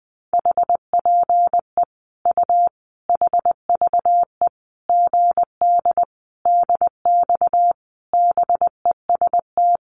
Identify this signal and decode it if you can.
Morse code, heard as a keyed tone, about 20 wpm, 700 Hz. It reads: HPE U HVE GD DX BEST
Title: Common QSO Phrases 20wpm